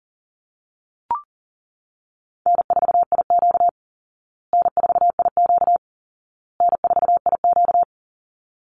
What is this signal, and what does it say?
Morse code, heard as a keyed tone, about 40 wpm, 700 Hz. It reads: N4IQ N4IQ N4IQ
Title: Call Signs Repeated 2x - 40wpm